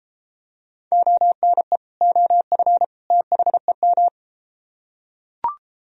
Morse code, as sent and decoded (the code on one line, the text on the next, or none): --- -. / --- ..-. / - .... --
ONE OF THEM